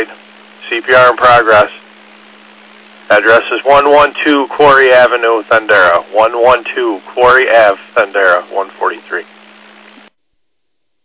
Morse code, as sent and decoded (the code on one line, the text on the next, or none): none